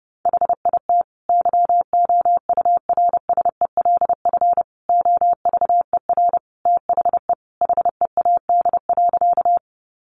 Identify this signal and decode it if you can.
Morse code, heard as a keyed tone, about 30 wpm, 700 Hz. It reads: HIT YOURSELF OVER THE HEAD.
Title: Sentences from Top 1200 Words 30wpm